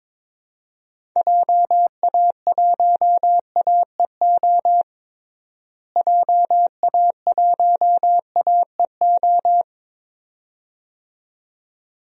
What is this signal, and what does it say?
Morse code, heard as a keyed tone, about 22 wpm, 700 Hz. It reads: JA1AEO JA1AEO
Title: DXCC-100 Call Signs - 6 Characters - Repeated 2x - 22wpm